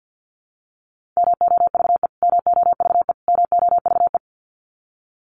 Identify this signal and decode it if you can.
Morse code, heard as a keyed tone, about 50 wpm, 700 Hz. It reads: MOVE MOVE MOVE
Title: Top 500 Words Repeated 50wpm